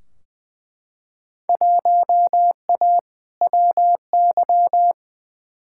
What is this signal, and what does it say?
Morse code, heard as a keyed tone, about 20 wpm, 700 Hz. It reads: 1A WY
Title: Field Day Exchanges Encore - 20wpm